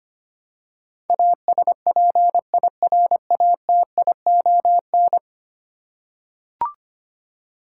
Morse code, heard as a keyed tone, about 25 wpm, 700 Hz. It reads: ASPIRATION
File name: Ten Letter Words - 25wpm